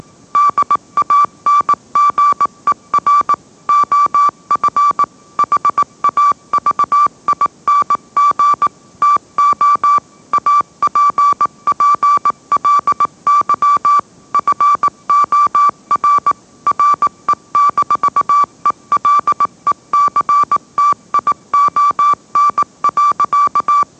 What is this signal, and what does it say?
Morse code, heard as a keyed tone, about 18 wpm, 1160 Hz. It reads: DANGER OF HAVING TO APPLY FOR RE-ELECTION.